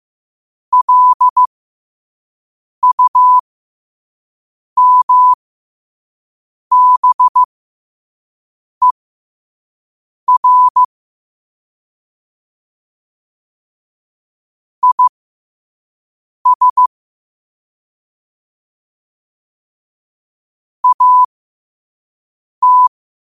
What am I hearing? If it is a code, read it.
Morse code, heard as a keyed tone, about 15 wpm, spaced as for 5 wpm, 1000 Hz. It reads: LUMBER IS AT
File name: qotdfiles / 5wpm